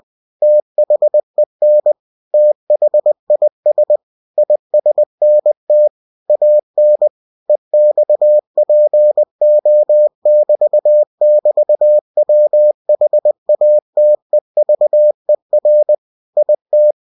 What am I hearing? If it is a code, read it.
Morse code, heard as a keyed tone, about 20 wpm, 600 Hz. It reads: THEN THIS ISNT AN EXPO<BT><BT>WHATEVER IT
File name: cw-books / winnie-the-pooh / winnie-20wpm-page-0079